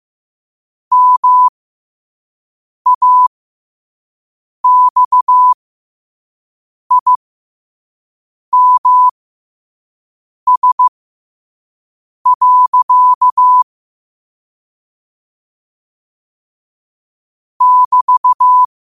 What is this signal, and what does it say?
Morse code, heard as a keyed tone, about 15 wpm, spaced as for 5 wpm, 1000 Hz. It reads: MAXIMS. <BT>
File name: qotdfiles / 5wpm